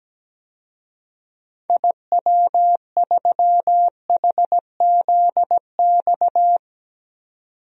Morse code, heard as a keyed tone, about 17 wpm, 700 Hz. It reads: IW3HZX
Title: DXCC-100 Call Signs - 6 Characters - 17wpm